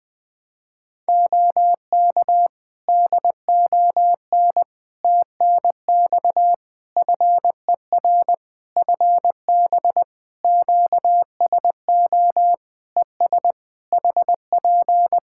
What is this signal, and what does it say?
Morse code, heard as a keyed tone, about 20 wpm, 700 Hz. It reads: OK DON TNX FER FB QSO ES HP
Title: Practice QSO - Exchange 4 - Ending - 20wpm